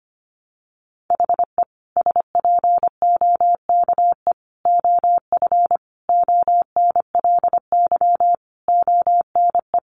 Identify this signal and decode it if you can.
Morse code, heard as a keyed tone, about 25 wpm, 700 Hz. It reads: HE SPOKE OF ONLY ONE
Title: Sentences from Top 1200 Words 25wpm